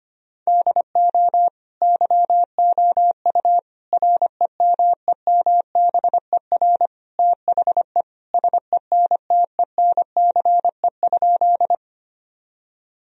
Morse code, heard as a keyed tone, about 25 wpm, 700 Hz. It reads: DO YOU REMEMBER THE SENTENCE?
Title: Sentences from Top 1000 Words - Review Part 1 - 25wpm